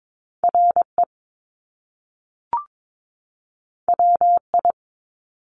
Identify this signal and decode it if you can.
Morse code, heard as a keyed tone, about 22 wpm, 700 Hz. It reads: RE WI
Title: Top Two Letter Combinations in Words 22wpm